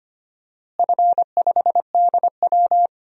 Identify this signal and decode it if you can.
Morse code, heard as a keyed tone, about 25 wpm, 700 Hz. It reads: F5DW
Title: DXCC-100 Call Signs - 4 Characters - Repeated 3x - 25wpm